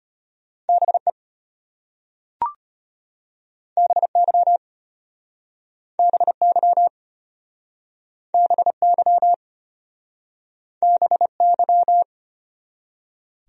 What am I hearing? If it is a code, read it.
Morse code, heard as a keyed tone, about 33 wpm, 700 Hz. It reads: BE BY BY BY BY